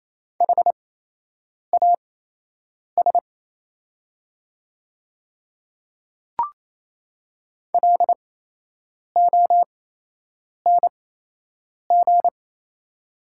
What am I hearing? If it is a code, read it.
Morse code, heard as a keyed tone, about 27 wpm, spaced as for 8 wpm, 700 Hz. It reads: HAS LONG
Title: 8x Character Spacing - Top 100 Words - 28wpm